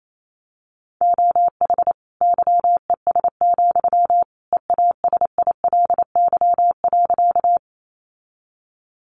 Morse code, heard as a keyed tone, about 28 wpm, 700 Hz. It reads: OH YES, EASILY.
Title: Sentences from Top 900 Words 28wpm